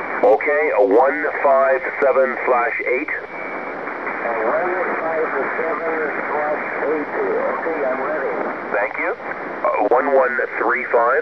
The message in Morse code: none